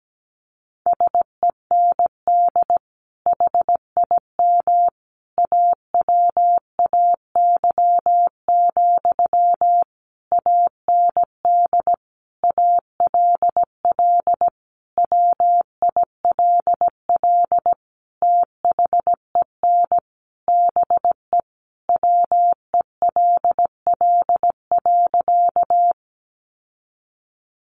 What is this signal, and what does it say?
Morse code, heard as a keyed tone, about 17 wpm, 700 Hz. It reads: SEND HIM AWAY, AND ALL WILL THEN BE WELL.
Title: Sentences from Top 1500 Words 17wpm